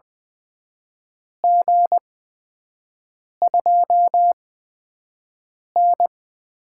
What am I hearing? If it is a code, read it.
Morse code, heard as a keyed tone, about 20 wpm, spaced as for 5 wpm, 700 Hz. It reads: G2N
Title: USA Call Signs - 4 Characters - 8x Character Spacing - 20wpm